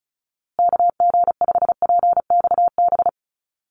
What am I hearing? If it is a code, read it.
Morse code, heard as a keyed tone, about 34 wpm, 700 Hz. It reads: KG5PXB
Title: USA Call Signs Repeated 2x 35wpm